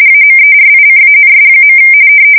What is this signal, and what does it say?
RTTY, about 45.45 baud. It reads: AMERICANS.
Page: 170 Shift 2125/2295